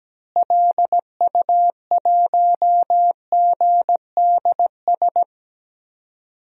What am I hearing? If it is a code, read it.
Morse code, heard as a keyed tone, about 17 wpm, 700 Hz. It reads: LU1GDS
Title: DXCC-100 Call Signs - 6 Characters - Repeated 3x - 17wpm